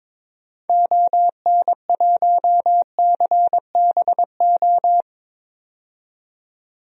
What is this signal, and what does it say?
Morse code, heard as a keyed tone, about 22 wpm, 700 Hz. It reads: ON1CBO
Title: DXCC-100 Call Signs - 6 Characters - 22wpm